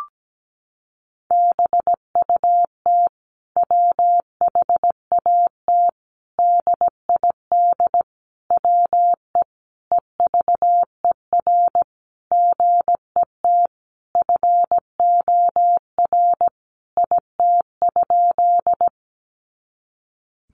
Morse code, heard as a keyed tone, about 17 wpm, 700 Hz. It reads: BUT WHAT DID WE EVER GET FOR IT?
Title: Sentences from Top 400 Words 17wpm